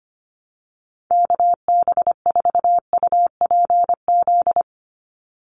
Morse code, heard as a keyed tone, about 25 wpm, 690 Hz. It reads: KB4UPZ
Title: USA Call Signs 25wpm